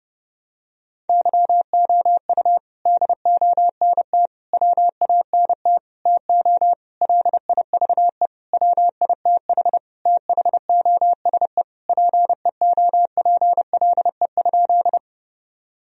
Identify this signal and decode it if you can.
Morse code, heard as a keyed tone, about 29 wpm, 700 Hz. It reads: YOU DONT WANT TO LIVE WITH THOSE PEOPLE?
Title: Sentences from Top 1000 Words - Review Part 1 - 30wpm